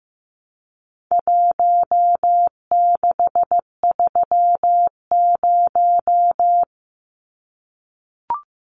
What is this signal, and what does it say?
Morse code, heard as a keyed tone, about 15 wpm, 700 Hz. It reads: 1630
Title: Numbers - Contest Style - 4-Digit 15wpm